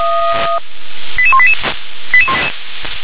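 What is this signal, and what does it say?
Five-tone selective call (ZVEI), heard as quick runs of five tones, tones of about 70 ms. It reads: 88188
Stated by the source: ZVEI five-tone call received on SDR.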